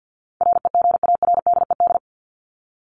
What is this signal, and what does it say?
Morse code, heard as a keyed tone, about 50 wpm, 700 Hz. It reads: REGARDED
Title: ED Suffix - Top 100 Words with Base Words 50wpm